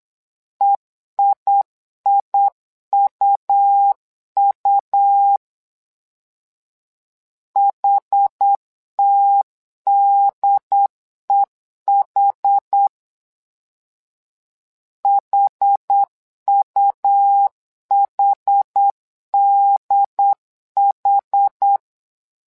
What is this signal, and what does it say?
Morse code, heard as a keyed tone, about 8 wpm, 800 Hz. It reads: EIIUU HTDEH HUHDH
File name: LEZIONI / 2 UHDN / 2 UHD 8wpm 2258